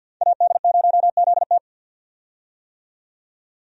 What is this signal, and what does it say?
Morse code, heard as a keyed tone, about 52 wpm, 700 Hz. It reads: AD0ZT
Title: USA Call Signs - 5 Characters - 50wpm